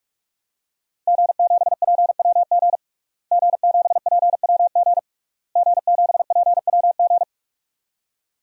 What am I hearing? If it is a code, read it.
Morse code, heard as a keyed tone, about 42 wpm, 700 Hz. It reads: G7PWG G7PWG G7PWG
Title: DXCC-100 Call Signs - 5 Characters - Repeated 3x - 45wpm